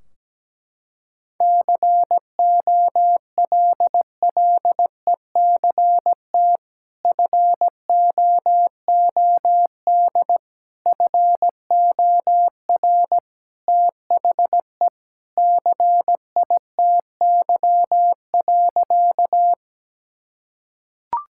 Morse code, heard as a keyed tone, about 17 wpm, 700 Hz. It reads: COLLECT FOOD FOR THE CITY.
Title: Sentences from Top 1600 Words 17wpm